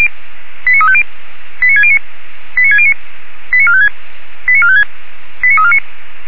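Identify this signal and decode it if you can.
Five-tone selective call (ZVEI), heard as quick runs of five tones, tones of about 70 ms. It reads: 89350 89609 89609 89456 89456 89349
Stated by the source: ZVEI five-tone selective call received by SDR, from audio server of feuerwehr-orlamuende.org (fire brigade Orlamünde).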